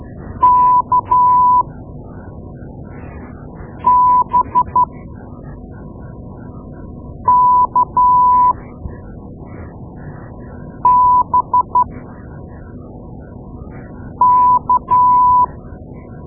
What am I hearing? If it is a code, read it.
Morse code, heard as a keyed tone, about 12 wpm, spaced as for 3 wpm, 990 Hz. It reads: KBKBK
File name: Morse Lessons / Morse Lessons / Tape-2A